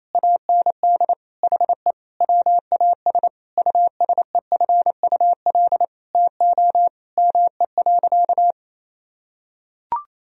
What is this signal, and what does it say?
Morse code, heard as a keyed tone, about 28 wpm, 700 Hz. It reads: AND HE WAS USEFUL TO ME.
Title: Sentences from Top 1000 Words - Review Part 4 - 28wpm